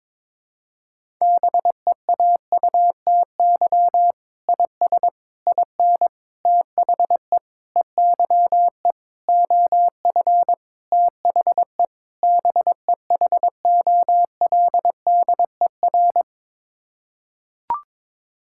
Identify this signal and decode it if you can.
Morse code, heard as a keyed tone, about 21 wpm, 700 Hz. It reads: BEAUTY IS IN THE EYE OF THE BEHOLDER
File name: practice / 22wpm / Sets of 8 Words - 22wpm